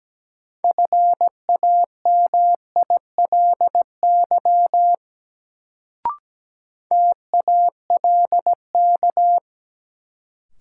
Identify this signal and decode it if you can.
Morse code, heard as a keyed tone, about 17 wpm, 700 Hz. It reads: FAMILY TALK
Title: Sentences from Top 400 Words - New Word Drill - 17wpm